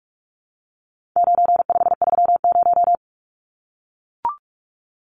Morse code, heard as a keyed tone, about 44 wpm, 700 Hz. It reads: 9530
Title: Numbers - 4-Digit 45wpm